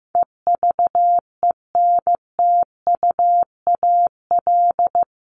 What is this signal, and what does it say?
Morse code, heard as a keyed tone, about 15 wpm, 700 Hz. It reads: EVENTUAL
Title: LY Suffix - Top 100 Words with Base Words 15wpm